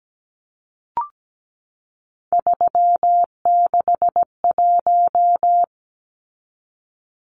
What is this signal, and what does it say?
Morse code, heard as a keyed tone, about 17 wpm, 700 Hz. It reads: 361
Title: Numbers - 3-Digit 17wpm